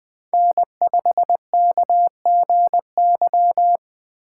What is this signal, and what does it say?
Morse code, heard as a keyed tone, about 21 wpm, 700 Hz. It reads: N5KGY
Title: 2023 POTA Hunter Call Signs - 20wpm